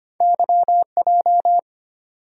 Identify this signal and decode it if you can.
Morse code, heard as a keyed tone, about 25 wpm, 700 Hz. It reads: YJ
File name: Two Letter-Number Combinations - 25wpm